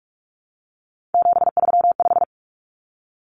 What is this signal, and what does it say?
Morse code, heard as a keyed tone, about 44 wpm, 700 Hz. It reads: 735
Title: Common Cut Numbers - 3-Digit 45wpm